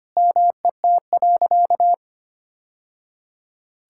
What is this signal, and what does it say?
Morse code, heard as a keyed tone, about 25 wpm, 700 Hz. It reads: MET.
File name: practice / 25wpm / Sentences with 3-Word Phrases - 25wpm